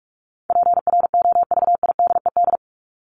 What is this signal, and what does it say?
Morse code, heard as a keyed tone, about 44 wpm, 700 Hz. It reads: PROVIDED
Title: ED Suffix - Top 100 Words with Base Words 45wpm